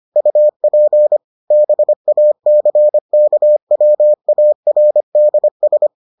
Morse code, heard as a keyed tone, about 25 wpm, 600 Hz. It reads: UP BACKWARDS